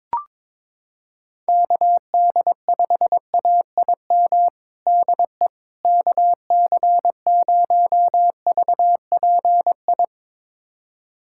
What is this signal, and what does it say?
Morse code, heard as a keyed tone, about 22 wpm, 700 Hz. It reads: KD5AIM DE KC0VPI
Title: USA Callsign DE Callsign 22wpm